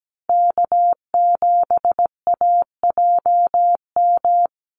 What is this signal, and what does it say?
Morse code, heard as a keyed tone, about 17 wpm, 700 Hz. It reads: K7AJM
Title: Call Signs - 17wpm